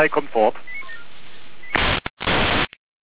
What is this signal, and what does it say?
Five-tone selective call (ZVEI), heard as quick runs of five tones, tones of about 70 ms, with speech. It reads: 89174